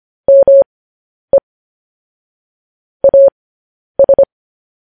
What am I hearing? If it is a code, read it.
Morse code, heard as a keyed tone, about 25 wpm, spaced as for 10 wpm, 550 Hz. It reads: ME AS